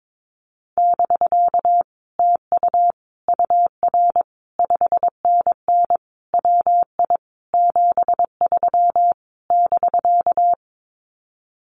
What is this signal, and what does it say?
Morse code, heard as a keyed tone, about 22 wpm, 700 Hz. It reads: <BK> TU UR 5NN WI 73 <BK>
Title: Common QSO Phrases Encore 22wpm